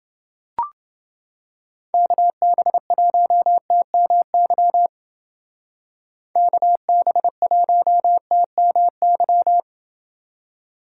Morse code, heard as a keyed tone, about 30 wpm, 700 Hz. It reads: KB1TMY KB1TMY